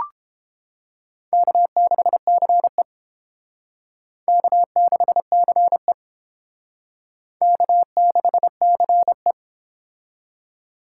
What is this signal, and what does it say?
Morse code, heard as a keyed tone, about 29 wpm, 700 Hz. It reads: K6CE K6CE K6CE